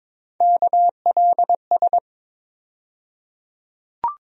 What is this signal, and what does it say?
Morse code, heard as a keyed tone, about 22 wpm, 700 Hz. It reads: KLS
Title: USA Call Sign Postfixes - 3 Letters 22wpm